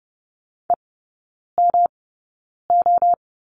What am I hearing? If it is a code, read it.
Morse code, heard as a keyed tone, about 30 wpm, spaced as for 9 wpm, 700 Hz. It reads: EMO
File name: Lesson 19e - Letter w - Three Characters 30wpm